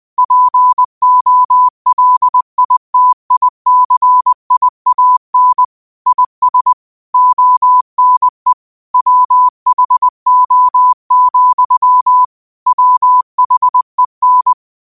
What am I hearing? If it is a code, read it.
Morse code, heard as a keyed tone, about 20 wpm, 1000 Hz. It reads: POLITICIAN IS ONE WHO, WHEN
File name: qotdfiles / 20wpm